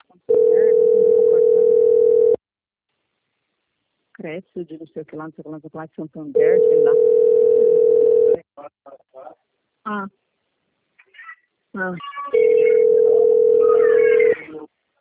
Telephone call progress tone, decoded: ringback tone

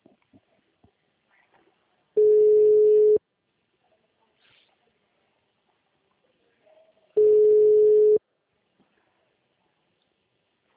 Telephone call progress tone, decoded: ringback tone

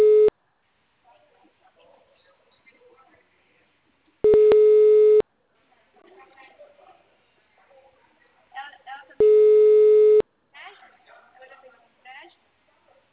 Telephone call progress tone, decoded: ringback tone

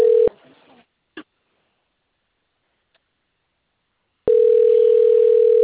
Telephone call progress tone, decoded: ringback tone

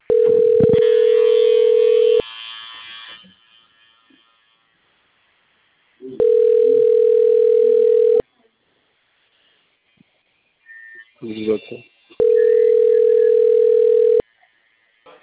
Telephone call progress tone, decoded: ringback tone

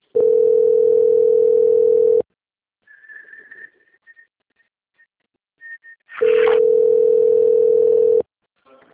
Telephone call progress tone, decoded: ringback tone

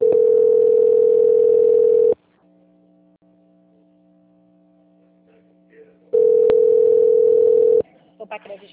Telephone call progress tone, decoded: ringback tone

